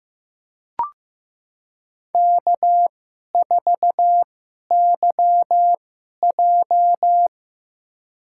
Morse code, heard as a keyed tone, about 15 wpm, spaced as for 11 wpm, 700 Hz. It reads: K4YJ
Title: USA Call Signs - 4 Characters - 2x Character Spacing - 15wpm